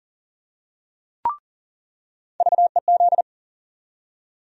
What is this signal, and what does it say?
Morse code, heard as a keyed tone, about 40 wpm, 700 Hz. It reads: VEZ
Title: USA Call Sign Postfixes - 3 Letters 40wpm